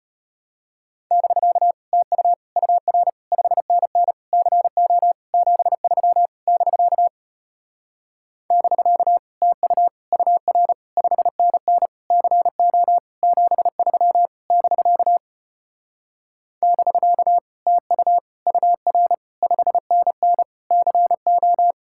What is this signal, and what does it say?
Morse code, heard as a keyed tone, about 33 wpm, 700 Hz. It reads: <BK> TU UR 5NN CO 73 <BK> <BK> TU UR 5NN CO 73 <BK> <BK> TU UR 5NN CO